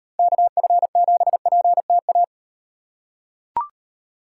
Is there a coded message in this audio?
Yes: Morse code from 0.2 to 2.2 s, heard as a keyed tone, about 38 wpm, 700 Hz. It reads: KF7PTA